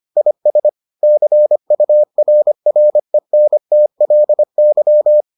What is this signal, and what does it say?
Morse code, heard as a keyed tone, about 25 wpm, 600 Hz. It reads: IS CURRENTLY